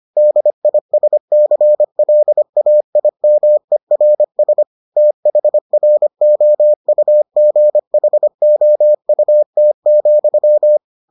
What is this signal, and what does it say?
Morse code, heard as a keyed tone, about 25 wpm, 600 Hz. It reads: DISCLAIMERS THROUGHOUT,